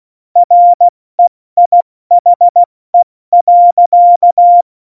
Morse code, heard as a keyed tone, about 16 wpm, 700 Hz. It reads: REIHE.
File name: Morsekurs / Witze / Witz 19. Januar 2021, 70 F80